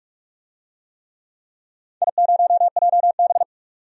Morse code, heard as a keyed tone, about 41 wpm, 700 Hz. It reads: I0JB
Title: DXCC-100 Call Signs - 4 Characters - 45wpm